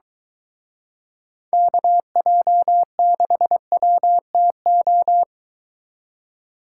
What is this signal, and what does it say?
Morse code, heard as a keyed tone, about 23 wpm, 700 Hz. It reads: KJ6WTO